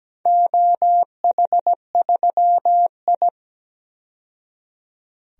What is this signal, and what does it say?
Morse code, heard as a keyed tone, about 17 wpm, 700 Hz. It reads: OH3I
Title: DXCC-100 Call Signs - 4 Characters - 17wpm